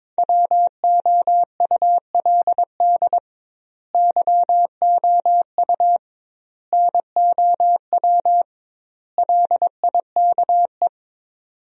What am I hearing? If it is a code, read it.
Morse code, heard as a keyed tone, about 22 wpm, 700 Hz. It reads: WOULD YOU NOW LIKE
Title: 2x Word-Spacing - Sentences from Top 200 Words - 22wpm